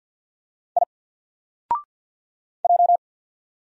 Morse code, heard as a keyed tone, about 50 wpm, 700 Hz. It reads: I J
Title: Single Character (Letter:Number:Punctuation:Prosign) 50wpm